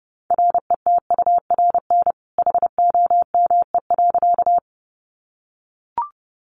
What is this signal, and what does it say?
Morse code, heard as a keyed tone, about 29 wpm, 700 Hz. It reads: RETURN HOME.
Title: Sentences from Top 800 Words 30wpm